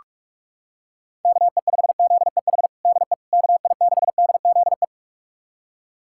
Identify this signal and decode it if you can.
Morse code, heard as a keyed tone, about 46 wpm, 700 Hz. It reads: KE5ZEH DE KI6DZE
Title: USA Callsign DE Callsign 45wpm